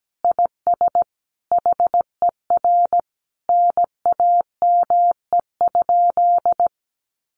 Morse code, heard as a keyed tone, about 17 wpm, 700 Hz. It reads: IS HER NAME?
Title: Sentences from Top 200 Words 17wpm